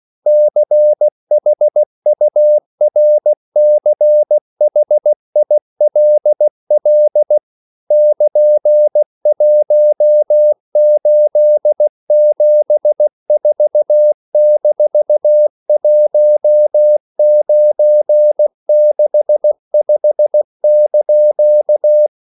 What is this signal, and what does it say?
Morse code, heard as a keyed tone, about 16 wpm, 600 Hz. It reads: CHURCHILL <KN>1874-1965)
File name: CW / Practice / EnglishQuotations0416